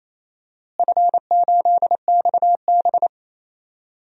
Morse code, heard as a keyed tone, about 28 wpm, 700 Hz. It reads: F8XB
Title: DXCC-100 Call Signs - 4 Characters - Repeated 3x - 28wpm